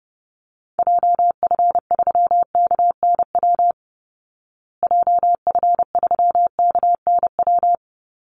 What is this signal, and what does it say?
Morse code, heard as a keyed tone, about 29 wpm, 700 Hz. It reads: JF3KNW JF3KNW
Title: Call Signs Repeated 3x - 30wpm